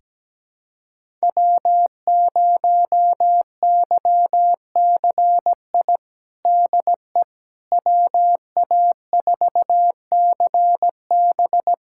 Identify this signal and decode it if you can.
Morse code, heard as a keyed tone, about 17 wpm, 700 Hz. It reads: W0YCI DE WA4CB
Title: USA Callsign DE Callsign - 5 Characters - 17wpm